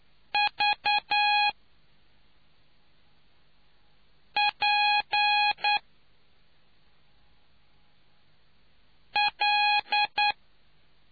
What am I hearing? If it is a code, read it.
Morse code, heard as a keyed tone, about 9 wpm, spaced as for 3 wpm, 830 Hz. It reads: VPL